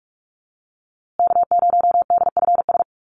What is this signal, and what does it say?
Morse code, heard as a keyed tone, about 44 wpm, 700 Hz. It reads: K0DFS
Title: USA Call Signs Repeated 3x 45wpm